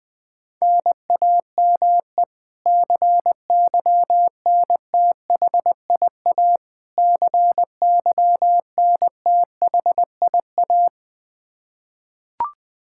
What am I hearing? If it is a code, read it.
Morse code, heard as a keyed tone, about 20 wpm, 700 Hz. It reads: NAME CYNTHIA CYNTHIA
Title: K1USN SST QSO Practice 20wpm